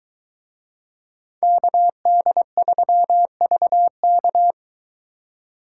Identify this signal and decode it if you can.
Morse code, heard as a keyed tone, about 22 wpm, 700 Hz. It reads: KD3VK